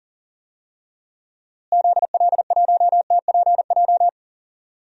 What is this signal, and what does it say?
Morse code, heard as a keyed tone, about 40 wpm, 700 Hz. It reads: ZL1TPJ
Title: DXCC-100 Call Signs - 6 Characters - Repeated 3x - 40wpm